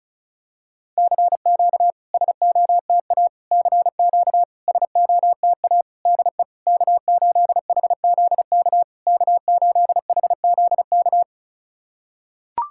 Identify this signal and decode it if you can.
Morse code, heard as a keyed tone, about 36 wpm, 700 Hz. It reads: CQ SOTA CQ SOTA DE K8HZK K8HZK
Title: SOTA QSO Practice 35wpm